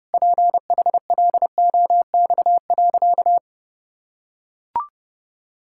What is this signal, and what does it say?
Morse code, heard as a keyed tone, about 30 wpm, 700 Hz. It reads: PHLOX.
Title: Pangram Challenge v2 - 30wpm